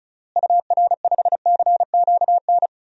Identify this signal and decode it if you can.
Morse code, heard as a keyed tone, about 36 wpm, 700 Hz. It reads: UR5CQN
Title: DXCC-100 Call Signs - 6 Characters - Repeated 3x - 35wpm